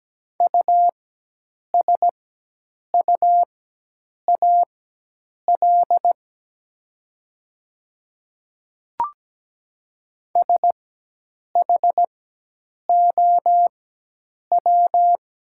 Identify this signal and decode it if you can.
Morse code, heard as a keyed tone, about 17 wpm, spaced as for 8 wpm, 700 Hz. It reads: USUAL SHOW
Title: 4x Character Spacing - Top 500 Words - 17wpm